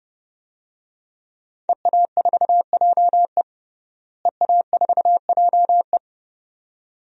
Morse code, heard as a keyed tone, about 30 wpm, 700 Hz. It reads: EA4JE EA4JE
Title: DXCC-100 Call Signs - 5 Characters - Repeated 2x - 30wpm